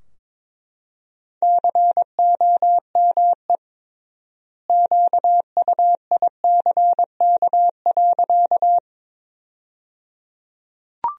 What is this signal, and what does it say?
Morse code, heard as a keyed tone, about 21 wpm, 700 Hz. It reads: COME QUICK.
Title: 3x Word-Spacing - Sentences from Top 500 Words - 22wpm